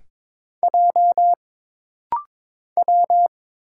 Morse code, heard as a keyed tone, about 22 wpm, 700 Hz. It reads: J W
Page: This rapid-fire timing is geared to push your ICR (Instant Character Recognition) skills.